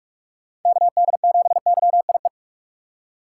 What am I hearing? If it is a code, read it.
Morse code, heard as a keyed tone, about 46 wpm, 700 Hz. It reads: KD7YIE
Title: USA Call Signs - 6 Characters - 2x Character Spacing - 45wpm